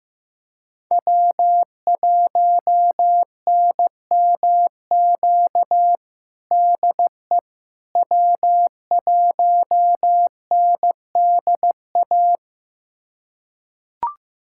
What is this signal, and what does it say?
Morse code, heard as a keyed tone, about 15 wpm, 700 Hz. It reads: W1NMQ DE W1NDA
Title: USA Callsign DE Callsign 15wpm